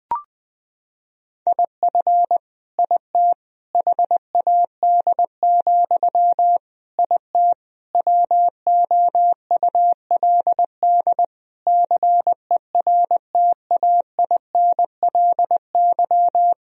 Morse code, heard as a keyed tone, about 20 wpm, 700 Hz. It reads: IF IT HAD, IT WOULD CERTAINLY
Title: Sentences from Top 1000 Words - Review Part 3 - 20wpm